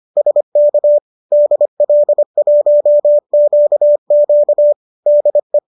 Morse code, heard as a keyed tone, about 25 wpm, 590 Hz. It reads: SK DL1QQ DE